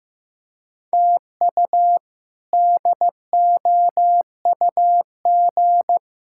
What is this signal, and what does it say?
Morse code, heard as a keyed tone, about 16 wpm, 700 Hz. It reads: TU DOUG
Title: K1USN SST QSO Practice - 15wpm